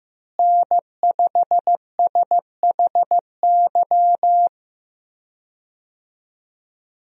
Morse code, heard as a keyed tone, about 16 wpm, 700 Hz. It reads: N5SHY
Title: USA Call Signs - 5 Characters - Repeated 2x - 15wpm